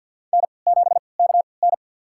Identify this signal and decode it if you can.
Morse code, heard as a keyed tone, about 53 wpm, spaced as for 29 wpm, 700 Hz. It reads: N7KN
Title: USA Call Signs - 3x Character Spacing - 50wpm